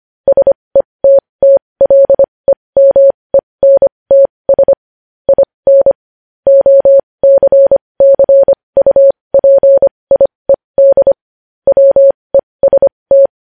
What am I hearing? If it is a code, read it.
Morse code, heard as a keyed tone, about 25 wpm, spaced as for 20 wpm, 550 Hz. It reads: SETTLEMENTS IN OCCUPIED WEST
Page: news-headlines-in-morse-code-at-20-wpm